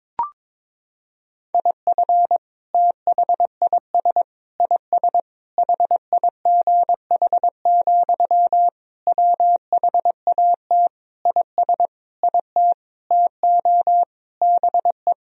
Morse code, heard as a keyed tone, about 22 wpm, 700 Hz. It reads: IF THIS IS HIGH, WHAT IS IT TO BE
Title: Sentences from Top 1000 Words - Review Part 1 - 22wpm